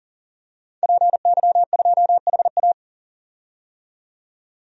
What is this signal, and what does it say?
Morse code, heard as a keyed tone, about 40 wpm, 700 Hz. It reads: PY2HA
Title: DXCC-100 Call Signs - 5 Characters - 40wpm